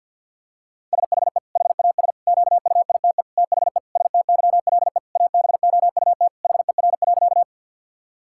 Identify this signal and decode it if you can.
Morse code, heard as a keyed tone, about 50 wpm, 700 Hz. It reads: SHE HAS QUITE THE STYLE ABOUT HER.
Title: Sentences from Top 1000 Words - Review Part 5 - 50wpm